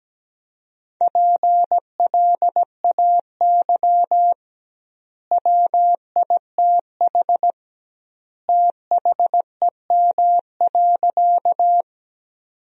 Morse code, heard as a keyed tone, about 17 wpm, 700 Hz. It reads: PLAY WITH THEM.
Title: 2x Word-Spacing - Sentences from Top 200 Words - 17wpm